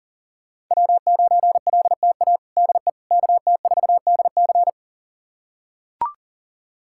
Morse code, heard as a keyed tone, about 40 wpm, 700 Hz. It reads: W9LTA DE KT4DC
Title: USA Callsign DE Callsign - 5 Characters - 40wpm